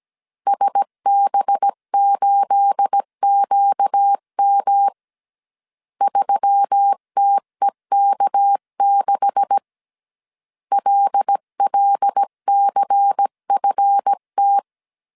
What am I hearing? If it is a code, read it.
Morse code, heard as a keyed tone, about 17 wpm, 800 Hz. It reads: SB8QM 3TEK6 LLCFT
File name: CW / DET / 1 9 1500C S B 16wpm 2856